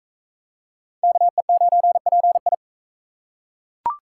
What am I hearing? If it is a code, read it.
Morse code, heard as a keyed tone, about 40 wpm, 700 Hz. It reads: KE9PI